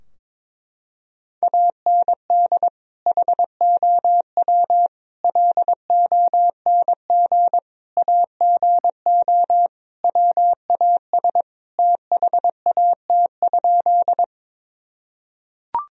Morse code, heard as a keyed tone, about 22 wpm, 700 Hz. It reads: AND HOW LONG AGO WAS THAT?
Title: Sentences from Top 1000 Words - Review Part 3 - 22wpm